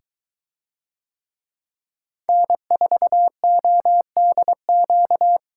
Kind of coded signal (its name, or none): Morse code